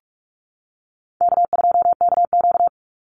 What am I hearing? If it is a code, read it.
Morse code, heard as a keyed tone, about 46 wpm, 700 Hz. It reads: K2KQ
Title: Easy Call Signs - 45wpm